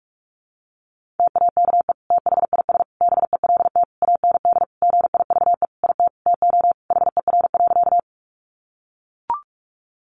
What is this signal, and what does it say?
Morse code, heard as a keyed tone, about 44 wpm, 700 Hz. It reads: TAKE THIS BELT AND GIVE IT TO HER.
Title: Sentences from Top 1400 Words 45wpm